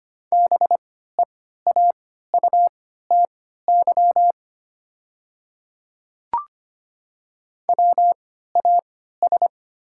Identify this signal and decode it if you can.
Morse code, heard as a keyed tone, about 25 wpm, spaced as for 14 wpm, 700 Hz. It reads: BEAUTY WAS
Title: 3x Character Spacing - Top 500 Words - 25wpm